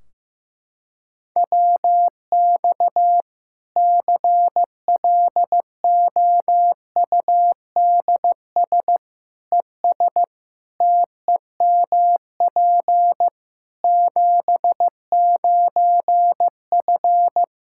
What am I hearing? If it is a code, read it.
Morse code, heard as a keyed tone, about 15 wpm, 700 Hz. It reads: WX CLOUDS ES TEMP 79F
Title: Practice QSO - Exchange 2 - Info - 15wpm